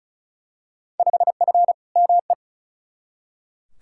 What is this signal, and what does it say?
Morse code, heard as a keyed tone, about 36 wpm, 700 Hz. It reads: 5F ME
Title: Field Day Exchanges Encore - 35wpm